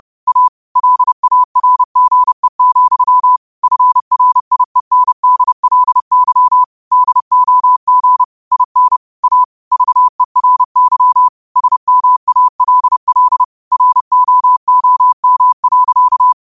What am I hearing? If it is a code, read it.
Morse code, heard as a keyed tone, about 30 wpm, 1000 Hz. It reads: A LARGE, FRIENDLY DOG IN A VERY SMALL ROOM.
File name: qotdfiles / 30wpm